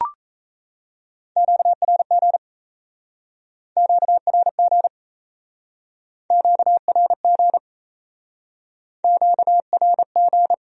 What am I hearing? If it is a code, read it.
Morse code, heard as a keyed tone, about 37 wpm, 700 Hz. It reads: QRG QRG QRG QRG